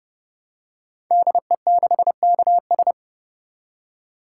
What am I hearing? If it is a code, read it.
Morse code, heard as a keyed tone, about 28 wpm, 700 Hz. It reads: DE6KS
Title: Code Groups - 5 Letter:Numbers - 30wpm